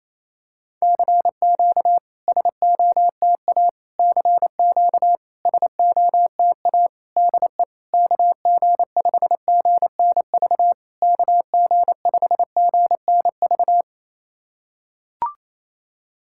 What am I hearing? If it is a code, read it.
Morse code, heard as a keyed tone, about 29 wpm, 700 Hz. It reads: CQ SOTA CQ SOTA DE KG5GNV KG5GNV
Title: SOTA QSO Practice 28wpm